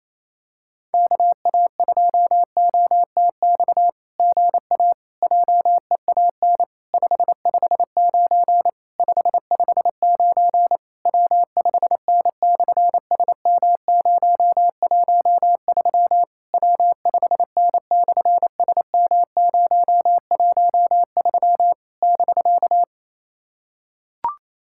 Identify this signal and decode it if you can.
Morse code, heard as a keyed tone, about 27 wpm, 700 Hz. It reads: KA2OTX GA JEAN 559 559 W5N/SM013 W5N/SM013 <BK>
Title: SOTA QSO Practice 28wpm